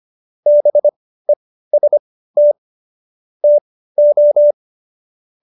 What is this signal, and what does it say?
Morse code, heard as a keyed tone, about 25 wpm, spaced as for 15 wpm, 590 Hz. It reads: BEST TO